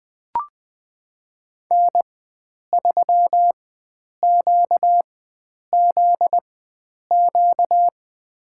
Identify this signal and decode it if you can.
Morse code, heard as a keyed tone, about 20 wpm, spaced as for 9 wpm, 700 Hz. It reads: N3QZQ
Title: USA Call Signs - 4x Character Spacing - 20wpm